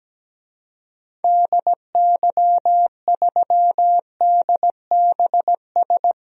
Morse code, heard as a keyed tone, about 17 wpm, 700 Hz. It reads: DY3DBS
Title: DXCC-100 Call Signs - 6 Characters - Repeated 2x - 17wpm